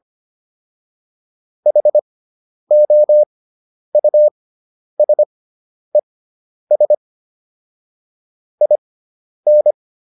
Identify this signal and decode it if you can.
Morse code, heard as a keyed tone, about 25 wpm, spaced as for 10 wpm, 600 Hz. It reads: HOUSES IN